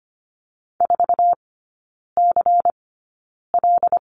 Morse code, heard as a keyed tone, about 25 wpm, spaced as for 9 wpm, 700 Hz. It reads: 4CL
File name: Lesson 17e - Letter u - Three Characters 25wpm